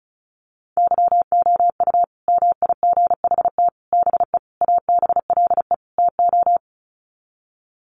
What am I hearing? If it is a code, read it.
Morse code, heard as a keyed tone, about 34 wpm, 700 Hz. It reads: YOU MIGHT BE ABLE TO
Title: Sets of 5 Words Encore2 35wpm